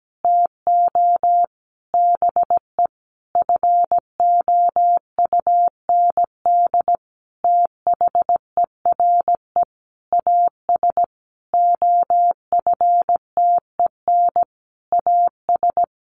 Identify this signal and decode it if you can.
Morse code, heard as a keyed tone, about 17 wpm, 700 Hz. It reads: TO BE FOUND THERE AS OFTEN AS
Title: Sentences from Top 1400 Words 17wpm